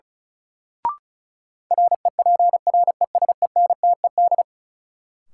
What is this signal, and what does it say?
Morse code, heard as a keyed tone, about 34 wpm, 700 Hz. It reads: REPRESENTED
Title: Sets of 5 Words - New Word Drill - 35wpm